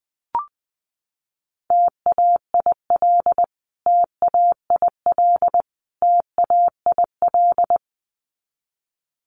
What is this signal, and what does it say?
Morse code, heard as a keyed tone, about 19 wpm, 700 Hz. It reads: TAIL TAIL TAIL
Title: Top 500 Words Repeated 20wpm